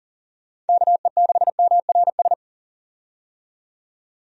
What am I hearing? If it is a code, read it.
Morse code, heard as a keyed tone, about 40 wpm, 700 Hz. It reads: KE6MRS
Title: USA Call Signs - 6 Characters - 40wpm